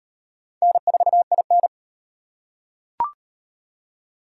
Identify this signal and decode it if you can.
Morse code, heard as a keyed tone, about 38 wpm, 700 Hz. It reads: N4IN